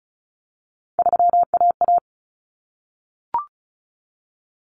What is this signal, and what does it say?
Morse code, heard as a keyed tone, about 34 wpm, 700 Hz. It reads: 3AA
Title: Common Cut Numbers - 3-Digit 35wpm